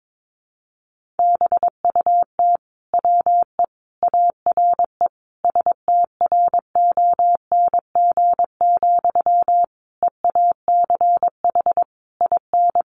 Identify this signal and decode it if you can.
Morse code, heard as a keyed tone, about 22 wpm, 700 Hz. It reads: BUT WE ARE STRONG, EACH IN
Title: Sentences from Top 800 Words 22wpm